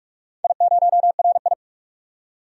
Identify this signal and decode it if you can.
Morse code, heard as a keyed tone, about 44 wpm, 700 Hz. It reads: I0RI
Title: DXCC-100 Call Signs - 4 Characters - 45wpm